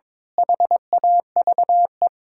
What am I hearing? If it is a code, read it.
Morse code, heard as a keyed tone, about 22 wpm, 700 Hz. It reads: HAVE
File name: practice / 22wpm / Build A Phrase - 5-Word Phrases - 22wpm